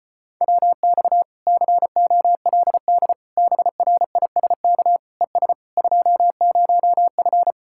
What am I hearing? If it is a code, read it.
Morse code, heard as a keyed tone, about 33 wpm, 700 Hz. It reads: WX COLD BRISK ES 20F